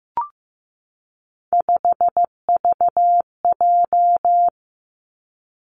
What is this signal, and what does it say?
Morse code, heard as a keyed tone, about 15 wpm, 700 Hz. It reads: 5VJ
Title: Three Challenging Letter-Number Combinations 15wpm